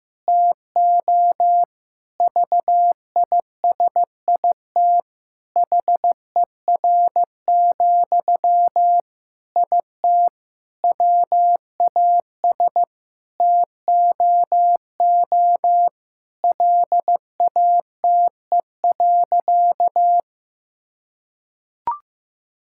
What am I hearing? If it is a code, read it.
Morse code, heard as a keyed tone, about 14 wpm, 700 Hz. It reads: TO VISIT HER, IT WAS TOO LATE.
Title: Sentences from Top 1000 Words - Review Part 6 - 15wpm